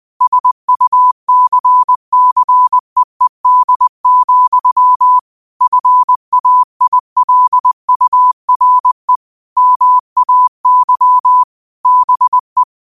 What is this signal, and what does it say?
Morse code, heard as a keyed tone, about 20 wpm, 1000 Hz. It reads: SUCCEED, FAILURE MAY BE
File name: qotdfiles / 20wpm